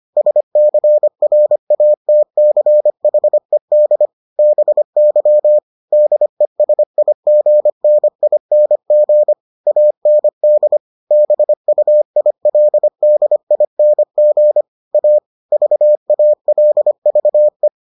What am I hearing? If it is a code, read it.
Morse code, heard as a keyed tone, about 25 wpm, 600 Hz. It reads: SCRATCHED BY DESIGNING AND BUILDING A VALVE